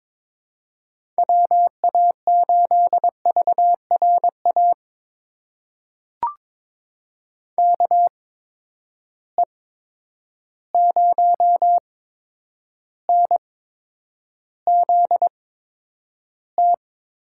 Morse code, heard as a keyed tone, about 21 wpm, spaced as for 9 wpm, 700 Hz. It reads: WA8VRA KE0NZT
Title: USA Call Signs - 6 Characters - 8x Character Spacing - 22wpm